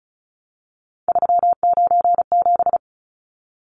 Morse code, heard as a keyed tone, about 34 wpm, 700 Hz. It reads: 397
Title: Numbers - 3-Digit 35wpm